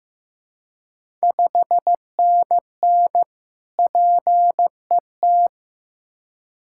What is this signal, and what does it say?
Morse code, heard as a keyed tone, about 15 wpm, 700 Hz. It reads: 5NN PET
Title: Ontario QSO Party Exchange 15wpm